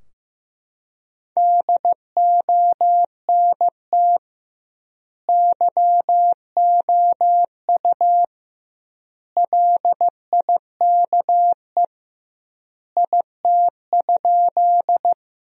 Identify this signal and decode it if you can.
Morse code, heard as a keyed tone, about 16 wpm, 700 Hz. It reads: DONT YOU LIKE IT?
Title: 2x Word-Spacing - Sentences from Top 300 Words - 15wpm